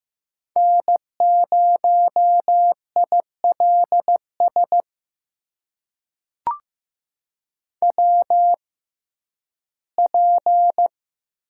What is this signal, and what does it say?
Morse code, heard as a keyed tone, about 15 wpm, spaced as for 7 wpm, 700 Hz. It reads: N0IL SWP
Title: USA Call Signs - 6x Character Spacing - 15wpm